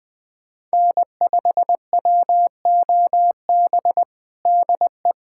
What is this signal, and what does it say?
Morse code, heard as a keyed tone, about 20 wpm, 700 Hz. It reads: N5WOB DE
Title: USA Callsign DE Callsign - 5 Characters - 20wpm